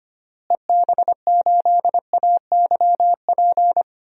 Morse code, heard as a keyed tone, about 25 wpm, 700 Hz. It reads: EB8AYP
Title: DXCC-100 Call Signs - 6 Characters - Repeated 3x - 25wpm